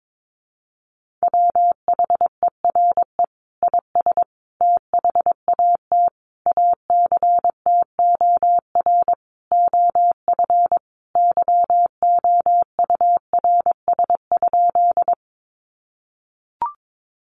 Morse code, heard as a keyed tone, about 22 wpm, 700 Hz. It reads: WHERE IS THAT ACTOR OF YOURS?
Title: Sentences from Top 1500 Words 22wpm